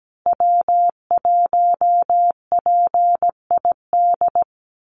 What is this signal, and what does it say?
Morse code, heard as a keyed tone, about 17 wpm, 700 Hz. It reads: W1PID
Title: Call Signs Repeated 3x - 17wpm